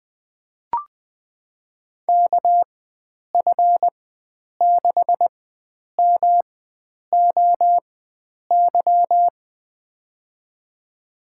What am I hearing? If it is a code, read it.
Morse code, heard as a keyed tone, about 20 wpm, spaced as for 9 wpm, 700 Hz. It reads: KF6MOY
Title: USA Call Signs - 6 Characters - 4x Character Spacing - 20wpm